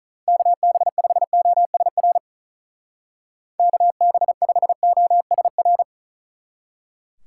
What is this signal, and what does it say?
Morse code, heard as a keyed tone, about 44 wpm, 700 Hz. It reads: KB5OSR KB5OSR